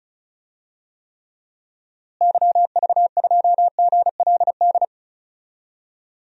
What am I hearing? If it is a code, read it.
Morse code, heard as a keyed tone, about 36 wpm, 700 Hz. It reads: YV2GLD
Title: DXCC-100 Call Signs - 6 Characters - Repeated 2x - 35wpm